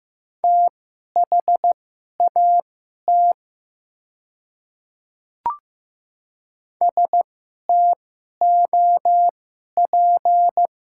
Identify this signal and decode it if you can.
Morse code, heard as a keyed tone, about 15 wpm, spaced as for 11 wpm, 700 Hz. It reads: THAT STOP
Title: 2x Character Spacing - Top 300 Words - 15wpm